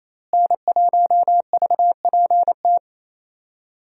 Morse code, heard as a keyed tone, about 27 wpm, 700 Hz. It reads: N1VPT
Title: USA Call Signs - 5 Characters - Repeated 3x - 28wpm